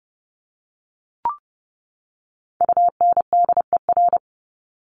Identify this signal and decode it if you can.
Morse code, heard as a keyed tone, about 30 wpm, 700 Hz. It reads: UNDER
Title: Top 300 Words 30wpm